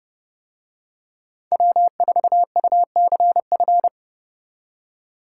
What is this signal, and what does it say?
Morse code, heard as a keyed tone, about 27 wpm, 700 Hz. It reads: W4UCF